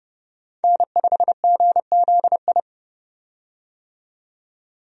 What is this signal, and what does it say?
Morse code, heard as a keyed tone, about 29 wpm, 700 Hz. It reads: N5GZI
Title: USA Call Signs - 5 Characters - 30wpm